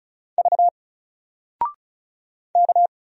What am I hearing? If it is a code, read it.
Morse code, heard as a keyed tone, about 36 wpm, 700 Hz. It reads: V K
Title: Single Letter-Number Alt 35wpm